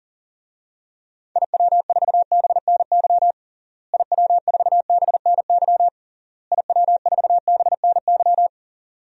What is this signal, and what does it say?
Morse code, heard as a keyed tone, about 37 wpm, 700 Hz. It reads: IW4BNY IW4BNY IW4BNY
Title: DXCC-100 Call Signs - 6 Characters - Repeated 3x - 40wpm